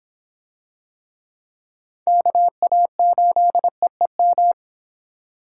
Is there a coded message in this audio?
Yes: Morse code from 2.1 to 4.5 s, heard as a keyed tone, about 25 wpm, 700 Hz. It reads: KA8EEM